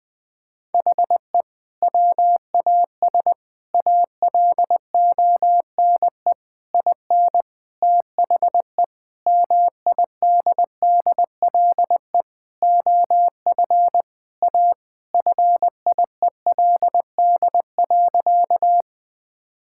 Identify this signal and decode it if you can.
Morse code, heard as a keyed tone, about 21 wpm, 700 Hz. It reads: HE WAS ALONE IN THE MIDDLE OF A FIELD.
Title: Sentences from Top 1000 Words - Review Part 6 - 20wpm